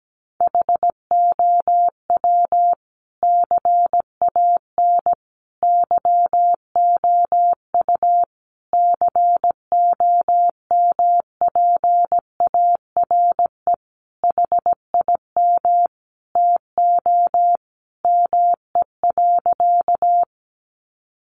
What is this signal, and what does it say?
Morse code, heard as a keyed tone, about 17 wpm, 700 Hz. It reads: HOW CAN YOU COMPARE HIM TO ME.
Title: Sentences from Top 1300 Words 17wpm